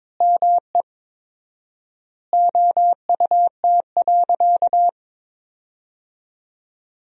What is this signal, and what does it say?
Morse code, heard as a keyed tone, about 22 wpm, 700 Hz. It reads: ME OUT.
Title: 4x Word-Spacing - Sentences from Top 500 Words - 22wpm